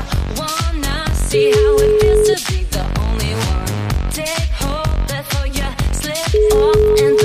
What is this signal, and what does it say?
Ringback tone (Europe, 425 Hz), with noise.